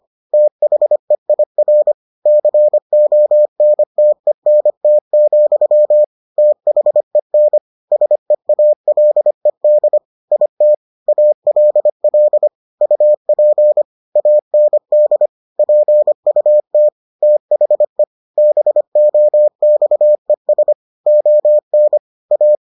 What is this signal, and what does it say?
Morse code, heard as a keyed tone, about 25 wpm, 600 Hz. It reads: THEIR CONTENT, THEN SEALED IT ALL UP AND PUT THE BOXES ON A